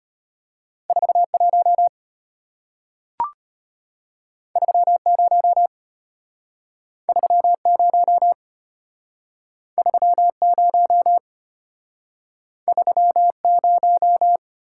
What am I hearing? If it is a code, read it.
Morse code, heard as a keyed tone, about 34 wpm, 700 Hz. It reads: 41 30 30 30 30